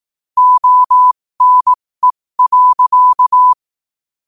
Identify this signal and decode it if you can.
Morse code, heard as a keyed tone, about 18 wpm, spaced as for 15 wpm, 1000 Hz. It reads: ONE.
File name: qotdfiles / 15wpm